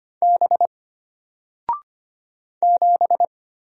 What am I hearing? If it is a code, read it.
Morse code, heard as a keyed tone, about 25 wpm, 700 Hz. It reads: B 7
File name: Single-Letter-Number-25wpm